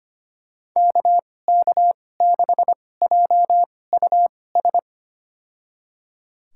Morse code, heard as a keyed tone, about 25 wpm, spaced as for 18 wpm, 700 Hz. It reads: KK6JUS
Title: USA Call Signs - 6 Characters - 2x Character Spacing - 25wpm